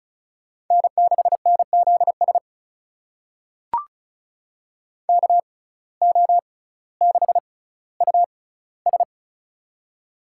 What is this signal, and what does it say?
Morse code, heard as a keyed tone, about 36 wpm, spaced as for 18 wpm, 700 Hz. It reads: N6NZS KO6US
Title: USA Call Signs - 5 Characters - 6x Character Spacing - 35wpm